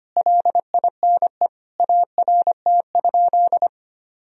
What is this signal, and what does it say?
Morse code, heard as a keyed tone, about 25 wpm, 700 Hz. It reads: LINE ART?
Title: Sentences from Top 1000 Words - Review Part 3 - 25wpm